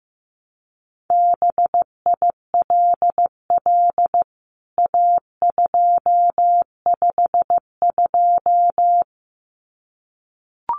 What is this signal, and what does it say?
Morse code, heard as a keyed tone, about 15 wpm, 700 Hz. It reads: BILL A252
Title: CWT Exchange 15wpm